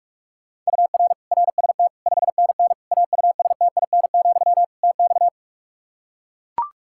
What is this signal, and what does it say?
Morse code, heard as a keyed tone, about 41 wpm, 700 Hz. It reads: UR RST 5NN AUSTIN, TX
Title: Practice QSO - Exchange 1 - Beginning - 45wpm